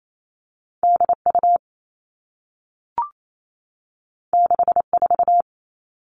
Morse code, heard as a keyed tone, about 28 wpm, 700 Hz. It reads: DU 64
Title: All Cut Numbers - 2-Digit 28wpm